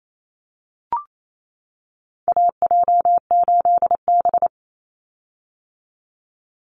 Morse code, heard as a keyed tone, about 28 wpm, 700 Hz. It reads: AJ8B
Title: Call Signs - 28wpm